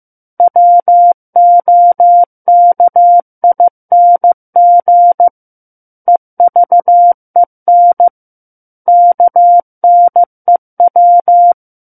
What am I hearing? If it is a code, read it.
Morse code, heard as a keyed tone, about 15 wpm, 700 Hz. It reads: WOKING EVEN KNEW
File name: Book One / 06-Chapter 6 - 15 WPM